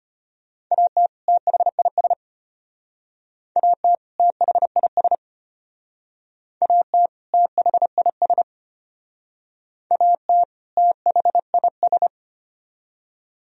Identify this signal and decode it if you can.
Morse code, heard as a keyed tone, about 32 wpm, 700 Hz. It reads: AT THIS AT THIS AT THIS AT THIS